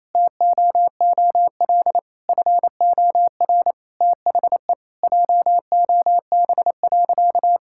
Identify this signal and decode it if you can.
Morse code, heard as a keyed tone, about 28 wpm, 700 Hz. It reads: TOOL FOR THE JOB.
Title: Sentences from Top 1000 Words - Review Part 6 - 28wpm